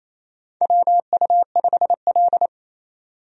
Morse code, heard as a keyed tone, about 27 wpm, 700 Hz. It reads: WU5L
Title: USA Call Signs - 4 Characters - 28wpm